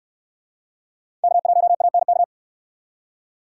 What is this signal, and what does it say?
Morse code, heard as a keyed tone, about 66 wpm, 700 Hz. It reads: K1ANK